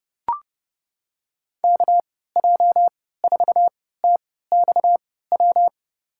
Morse code, heard as a keyed tone, about 32 wpm, spaced as for 17 wpm, 700 Hz. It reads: KJ4TXW
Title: USA Call Signs - 6 Characters - 3x Character Spacing - 30wpm